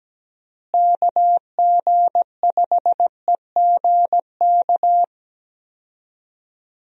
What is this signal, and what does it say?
Morse code, heard as a keyed tone, about 17 wpm, 700 Hz. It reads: KG5EGK
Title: USA Call Signs - 6 Characters - Repeated 2x - 17wpm